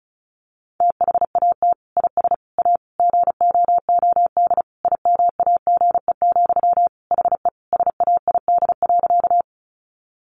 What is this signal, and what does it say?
Morse code, heard as a keyed tone, about 34 wpm, 700 Hz. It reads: THAT IS A GOOD IMAGE, HE SAID.
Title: Sentences from Top 700 Words 35wpm